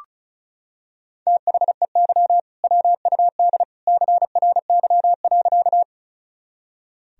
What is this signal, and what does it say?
Morse code, heard as a keyed tone, about 36 wpm, 700 Hz. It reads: THEY WUD CRY.
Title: Sentences from Top 500 Words and Abbreviations 35wpm